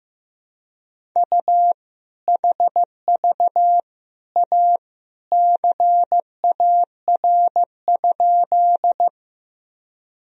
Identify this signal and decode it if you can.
Morse code, heard as a keyed tone, about 15 wpm, 700 Hz. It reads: U HV A CAR?
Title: Sentences from Top 300 Words and Abbreviations 15wpm